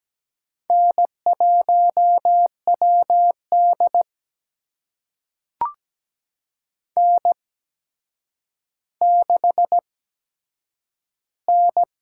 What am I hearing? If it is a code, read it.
Morse code, heard as a keyed tone, about 17 wpm, spaced as for 8 wpm, 700 Hz. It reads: N1WD N6N
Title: USA Call Signs - 4 Characters - 8x Character Spacing - 17wpm